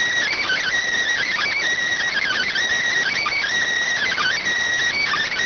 Emergency Alert System (SAME) tones, heard as harsh buzzing bursts. It is heard from 0.4 to 4.4 s.